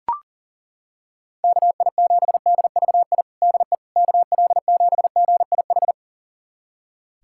Morse code, heard as a keyed tone, about 41 wpm, 690 Hz. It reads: KI7DVI DE KL7GIH
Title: USA Callsign DE Callsign - 6 Characters - 40wpm